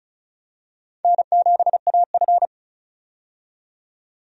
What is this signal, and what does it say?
Morse code, heard as a keyed tone, about 36 wpm, 700 Hz. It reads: N7AF
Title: USA Call Signs - 4 Characters - 35wpm